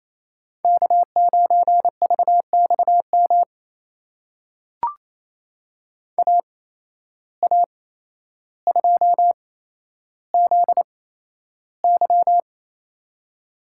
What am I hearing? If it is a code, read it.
Morse code, heard as a keyed tone, about 27 wpm, spaced as for 12 wpm, 700 Hz. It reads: K9VXM AA2ZY
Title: USA Call Signs - 5 Characters - 8x Character Spacing - 28wpm